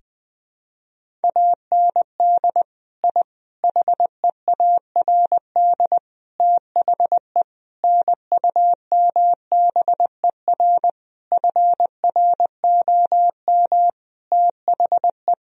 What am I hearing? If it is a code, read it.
Morse code, heard as a keyed tone, about 21 wpm, 700 Hz. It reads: AND I HEARD THE NUMBER FROM THE
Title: Sentences from Top 1000 Words - Review Part 1 - 20wpm